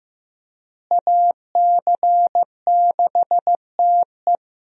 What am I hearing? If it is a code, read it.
Morse code, heard as a keyed tone, about 15 wpm, 700 Hz. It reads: AC6TE
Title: USA Call Signs - 5 Characters - Repeated 2x - 15wpm